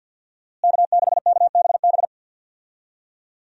Morse code, heard as a keyed tone, about 52 wpm, 700 Hz. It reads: K6KBB
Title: USA Call Signs - 5 Characters - Repeated 3x - 50wpm